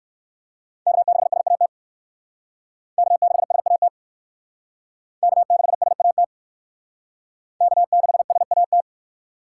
Morse code, heard as a keyed tone, about 55 wpm, 700 Hz. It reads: K6SAT K6SAT K6SAT K6SAT
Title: Speed Racing - USA 1x3 Call Signs 45wpm